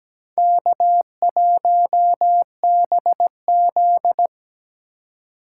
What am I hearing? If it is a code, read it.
Morse code, heard as a keyed tone, about 17 wpm, 700 Hz. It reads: K1BZ
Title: K1USN SST QSO Practice - 17wpm